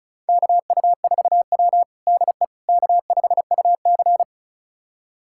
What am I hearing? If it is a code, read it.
Morse code, heard as a keyed tone, about 36 wpm, 690 Hz. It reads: KU4W DE K5UC
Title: USA Callsign DE Callsign - 4 Characters - 35wpm